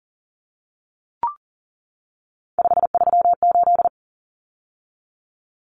Morse code, heard as a keyed tone, about 37 wpm, 700 Hz. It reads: 538
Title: Common Cut Numbers - 3-Digit 40wpm